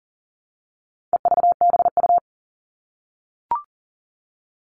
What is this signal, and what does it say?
Morse code, heard as a keyed tone, about 40 wpm, 700 Hz. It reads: EVBU
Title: All Cut Numbers - 4-Digit 40wpm